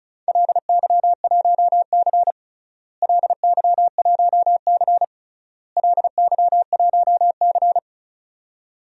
Morse code, heard as a keyed tone, about 36 wpm, 700 Hz. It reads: LY1C LY1C LY1C
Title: DXCC-100 Call Signs - 4 Characters - Repeated 3x - 35wpm